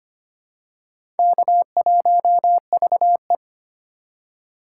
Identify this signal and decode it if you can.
Morse code, heard as a keyed tone, about 25 wpm, 700 Hz. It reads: K1VE
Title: USA Call Signs - 4 Characters - 2x Character Spacing - 25wpm